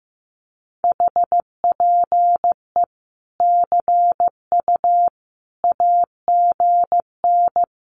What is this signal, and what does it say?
Morse code, heard as a keyed tone, about 15 wpm, 700 Hz. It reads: HPE CU AGN
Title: Common QSO Phrases 15wpm